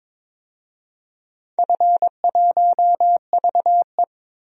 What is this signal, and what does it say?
Morse code, heard as a keyed tone, about 22 wpm, 700 Hz. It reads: F1VE
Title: DXCC-100 Call Signs - 4 Characters - 22wpm